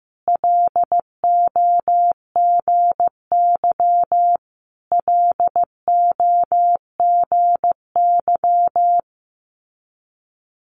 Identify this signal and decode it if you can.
Morse code, heard as a keyed tone, about 15 wpm, 700 Hz. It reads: LOGY LOGY
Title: Multi-letter Phonemes Repeated 3x 15wpm